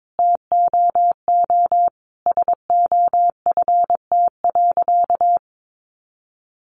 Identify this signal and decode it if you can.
Morse code, heard as a keyed tone, about 22 wpm, 700 Hz. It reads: TOO SOFT.
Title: Sentences from Top 1500 Words 22wpm